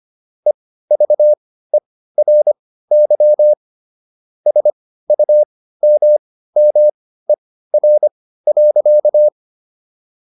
Morse code, heard as a keyed tone, about 25 wpm, spaced as for 15 wpm, 600 Hz. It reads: EVERY SUMMER.